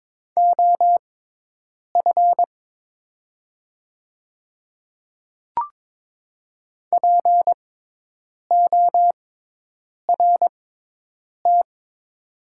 Morse code, heard as a keyed tone, about 22 wpm, spaced as for 8 wpm, 700 Hz. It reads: OF PORT
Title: 6x Character Spacing - Top 200 Words - 22wpm